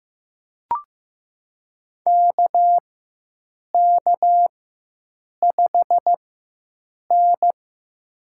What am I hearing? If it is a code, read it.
Morse code, heard as a keyed tone, about 15 wpm, spaced as for 7 wpm, 700 Hz. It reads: KK5N
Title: USA Call Signs - 4x Character Spacing - 15wpm